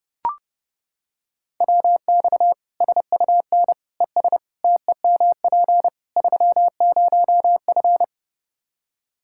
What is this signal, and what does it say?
Morse code, heard as a keyed tone, about 29 wpm, 700 Hz. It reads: WX SUN ES TEMP 30F
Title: Practice QSO - Exchange 2 - Info - 30wpm